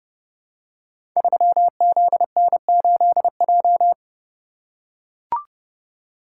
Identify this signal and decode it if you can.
Morse code, heard as a keyed tone, about 28 wpm, 700 Hz. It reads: 3ZN8J
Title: Code Groups - 5 Letter:Numbers - 30wpm